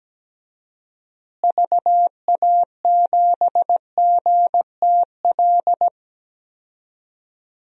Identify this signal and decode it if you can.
Morse code, heard as a keyed tone, about 17 wpm, 700 Hz. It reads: VA7GTL
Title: DXCC-100 Call Signs - 6 Characters - Repeated 3x - 17wpm